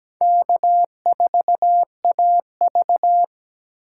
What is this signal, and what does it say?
Morse code, heard as a keyed tone, about 17 wpm, 700 Hz. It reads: K4AV
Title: USA Call Signs - 6x Character Spacing - 17wpm